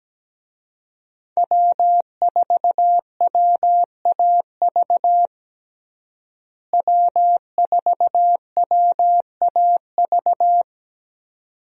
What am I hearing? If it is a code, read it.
Morse code, heard as a keyed tone, about 17 wpm, 700 Hz. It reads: W4WAV W4WAV
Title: USA Call Signs - 5 Characters - Repeated 2x - 17wpm